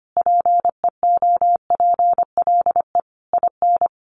Morse code, heard as a keyed tone, about 25 wpm, 700 Hz. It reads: PEOPLE IN
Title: Sets of 5 Words 25wpm